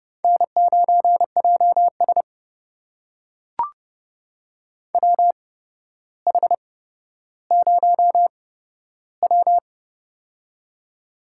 Morse code, heard as a keyed tone, about 30 wpm, spaced as for 13 wpm, 700 Hz. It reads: N9JS WH0W
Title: USA Call Signs - 4 Characters - 8x Character Spacing - 30wpm